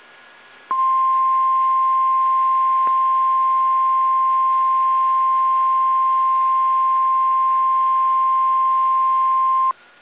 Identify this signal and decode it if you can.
Emergency Alert System (SAME) tones (the 1050 Hz attention tone). It is heard from 0.7 to 9.8 s.